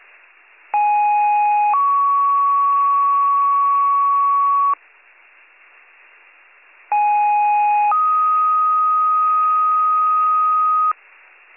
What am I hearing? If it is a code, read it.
Two-tone page, heard as a one-second tone, then a three-second tone, about 830 then 1120 Hz, about 830 then 1250 Hz. It is heard from 0.7 to 10.9 s.